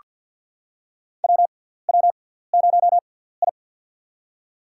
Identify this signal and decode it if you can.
Morse code, heard as a keyed tone, about 48 wpm, spaced as for 17 wpm, 700 Hz. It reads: WW0I
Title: USA Call Signs - 4 Characters - 6x Character Spacing - 50wpm